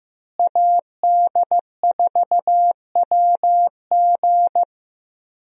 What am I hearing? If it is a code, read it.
Morse code, heard as a keyed tone, about 16 wpm, 700 Hz. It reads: AD4WG
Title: USA Call Signs - 5 Characters - Repeated 2x - 15wpm